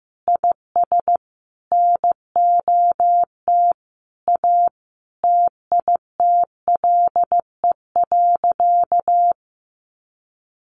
Morse code, heard as a keyed tone, about 15 wpm, 700 Hz. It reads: IS NOT A TITLE.
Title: Sentences from Top 1300 Words 15wpm